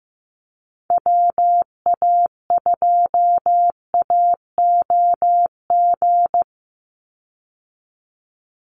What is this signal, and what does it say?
Morse code, heard as a keyed tone, about 15 wpm, 700 Hz. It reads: WA2AOG
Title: Call Signs Repeated 2x - 15wpm